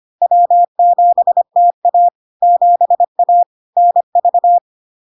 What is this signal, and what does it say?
Morse code, heard as a keyed tone, about 25 wpm, 700 Hz. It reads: W7TA 7A NV
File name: LARK-FD-2022-Log-in-CW-25wpm-700-Hz